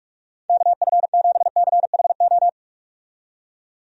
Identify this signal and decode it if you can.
Morse code, heard as a keyed tone, about 46 wpm, 700 Hz. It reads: KF7CHO
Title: USA Call Signs - 6 Characters - 45wpm